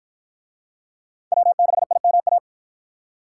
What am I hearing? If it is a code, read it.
Morse code, heard as a keyed tone, about 47 wpm, 700 Hz. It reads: W6IMA